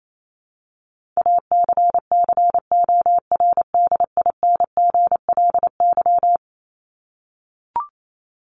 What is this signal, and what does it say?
Morse code, heard as a keyed tone, about 28 wpm, 700 Hz. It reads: ACCORDINGLY
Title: LY Suffix - Top 100 Words with Base Words 28wpm